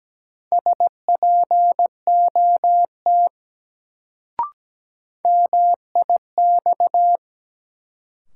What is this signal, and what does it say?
Morse code, heard as a keyed tone, about 17 wpm, 700 Hz. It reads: SPOT MIX
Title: Sentences from Top 1100 Words - New Word Drill Alt - 17wpm